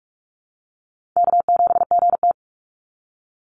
Morse code, heard as a keyed tone, about 44 wpm, 700 Hz. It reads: K7GT
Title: Call Signs - 45wpm